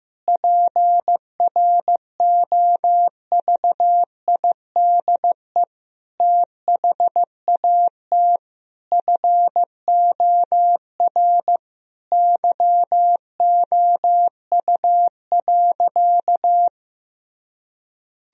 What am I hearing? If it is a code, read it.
Morse code, heard as a keyed tone, about 15 wpm, 700 Hz. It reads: PROVIDE THAT FOR YOU.
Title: Sentences from Top 1000 Words - Review Part 3 - 15wpm